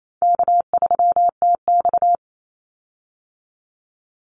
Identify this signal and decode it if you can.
Morse code, heard as a keyed tone, about 28 wpm, 700 Hz. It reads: K3TX
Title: Call Signs - 28wpm